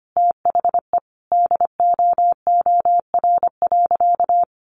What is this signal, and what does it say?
Morse code, heard as a keyed tone, about 25 wpm, 700 Hz. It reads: THE DOOR.
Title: Sentences from Top 500 Words 25wpm